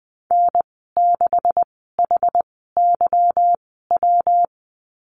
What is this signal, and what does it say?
Morse code, heard as a keyed tone, about 19 wpm, spaced as for 14 wpm, 700 Hz. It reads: N6HYW
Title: 161 Lesson 26g - Letter g - Call Signs 20wpm